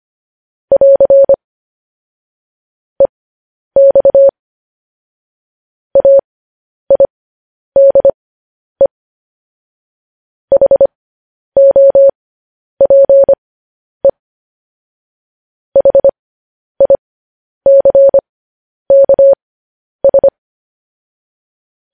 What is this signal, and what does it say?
Morse code, heard as a keyed tone, about 25 wpm, spaced as for 10 wpm, 550 Hz. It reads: <AR> EX AIDE HOPE HICKS
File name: podcast / News headlines-1x-25x10